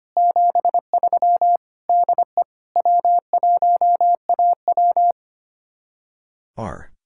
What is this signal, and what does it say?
Morse code, heard as a keyed tone, about 32 wpm, 700 Hz. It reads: 73 DE W1AW